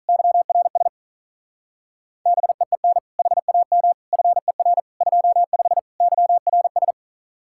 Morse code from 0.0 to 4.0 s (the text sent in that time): YRS BEEN HAM